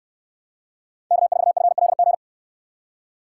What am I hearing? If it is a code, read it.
Morse code, heard as a keyed tone, about 59 wpm, 700 Hz. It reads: K4FLG